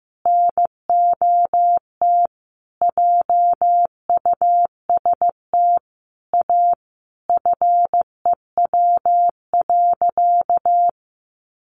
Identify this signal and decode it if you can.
Morse code, heard as a keyed tone, about 15 wpm, 700 Hz. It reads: NOT JUST A FEW.
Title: Sentences from Top 1500 Words 15wpm